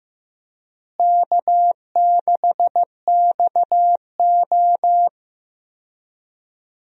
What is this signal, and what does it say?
Morse code, heard as a keyed tone, about 15 wpm, 700 Hz. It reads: K6XO
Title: USA Call Signs - 4 Characters - Repeated 3x - 15wpm